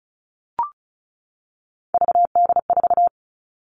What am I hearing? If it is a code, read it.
Morse code, heard as a keyed tone, about 34 wpm, 700 Hz. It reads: VD4
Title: All Cut Numbers - 3-Digit 35wpm